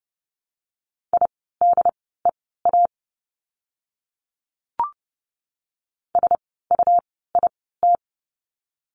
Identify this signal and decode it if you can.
Morse code, heard as a keyed tone, about 30 wpm, spaced as for 17 wpm, 700 Hz. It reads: IDEA SUIT
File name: Lesson 17f - Letter u - Words 30wpm